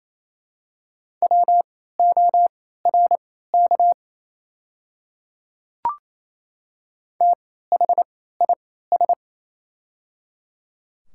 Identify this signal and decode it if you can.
Morse code, heard as a keyed tone, about 28 wpm, spaced as for 16 wpm, 700 Hz. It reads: WORK THIS
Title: 3x Character Spacing - Top 300 Words - 28wpm